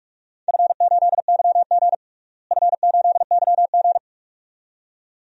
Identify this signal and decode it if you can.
Morse code, heard as a keyed tone, about 46 wpm, 700 Hz. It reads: F8YG F8YG
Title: DXCC-100 Call Signs - 4 Characters - Repeated 2x - 45wpm